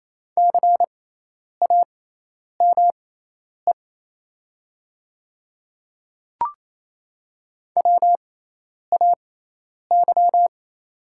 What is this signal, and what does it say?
Morse code, heard as a keyed tone, about 28 wpm, spaced as for 10 wpm, 700 Hz. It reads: CAME WAY
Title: 6x Character Spacing - Top 200 Words - 28wpm